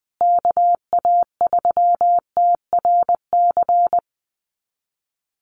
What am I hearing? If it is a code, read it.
Morse code, heard as a keyed tone, about 20 wpm, 700 Hz. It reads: KA3TRC
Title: USA Call Signs Repeated 3x 20wpm